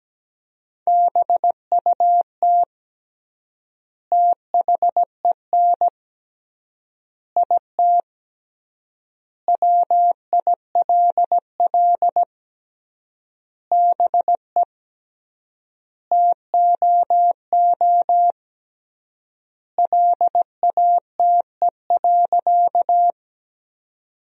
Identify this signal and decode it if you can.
Morse code, heard as a keyed tone, about 17 wpm, 700 Hz. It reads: BUT THEN IT WILL BE TOO LATE.
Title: 3x Word-Spacing - Sentences from Top 300 Words - 17wpm